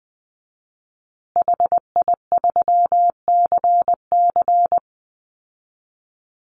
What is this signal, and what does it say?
Morse code, heard as a keyed tone, about 20 wpm, 700 Hz. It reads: HI3CC
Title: Call Signs Repeated 3x - 20wpm